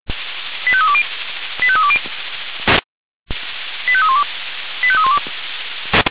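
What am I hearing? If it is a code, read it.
Five-tone selective call (ZVEI), heard as quick runs of five tones, tones of about 70 ms. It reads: 85320 85320 85312 85312
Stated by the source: ZVEI five-tone selective call received on SDR.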